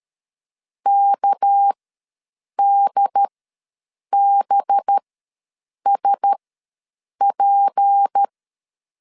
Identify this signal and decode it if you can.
Morse code, heard as a keyed tone, about 13 wpm, spaced as for 7 wpm, 800 Hz. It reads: KDBSP
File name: LEZIONI / 6 PKX0 / 6 PK ws 12wpm 3116